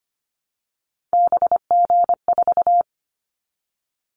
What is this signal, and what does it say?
Morse code, heard as a keyed tone, about 25 wpm, 700 Hz. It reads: BG4
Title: Callsign Prefixes for Top 100 DXCC - 25wpm